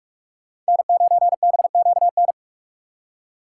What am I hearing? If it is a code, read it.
Morse code, heard as a keyed tone, about 58 wpm, 700 Hz. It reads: N9BQN